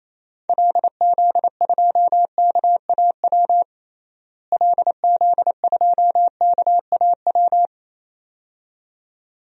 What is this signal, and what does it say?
Morse code, heard as a keyed tone, about 27 wpm, 700 Hz. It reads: LZ2KAW LZ2KAW
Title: DXCC-100 Call Signs - 6 Characters - Repeated 2x - 28wpm